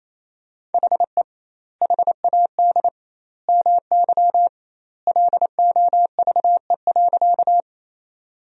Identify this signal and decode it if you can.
Morse code, heard as a keyed tone, about 28 wpm, 700 Hz. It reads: HE HAD MY LOVE.
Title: 2x Word-Spacing - Sentences from Top 400 Words - 28wpm